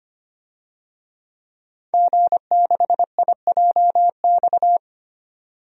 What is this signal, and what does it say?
Morse code, heard as a keyed tone, about 25 wpm, 700 Hz. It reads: G6IJX
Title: DXCC-100 Call Signs - 5 Characters - 25wpm